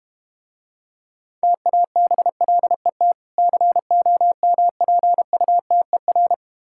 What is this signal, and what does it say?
Morse code, heard as a keyed tone, about 32 wpm, 700 Hz. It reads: TABLET COMPUTER